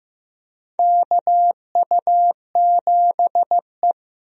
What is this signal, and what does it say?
Morse code, heard as a keyed tone, about 15 wpm, 700 Hz. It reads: KU7E
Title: USA Call Signs - 4 Characters - 15wpm